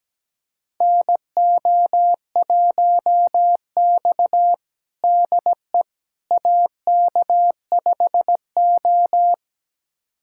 Morse code, heard as a keyed tone, about 17 wpm, 700 Hz. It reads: NO1X DE AK5O
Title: USA Callsign DE Callsign - 4 Characters - 17wpm